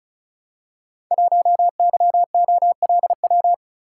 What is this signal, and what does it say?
Morse code, heard as a keyed tone, about 36 wpm, 700 Hz. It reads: 1YOLW
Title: Code Groups - 5 Letter:Numbers - 35wpm